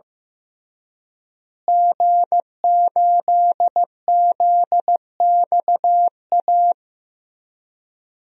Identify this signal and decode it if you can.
Morse code, heard as a keyed tone, about 15 wpm, 700 Hz. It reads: G8ZXA
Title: DXCC-100 Call Signs - 5 Characters - Repeated 3x - 15wpm